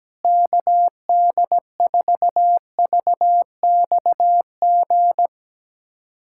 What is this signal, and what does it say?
Morse code, heard as a keyed tone, about 17 wpm, 700 Hz. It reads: KD4VXG
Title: USA Callsign DE Callsign 17wpm